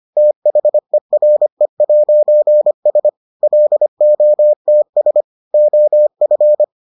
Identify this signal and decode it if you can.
Morse code, heard as a keyed tone, about 25 wpm, 600 Hz. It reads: THERE'S LOTS OF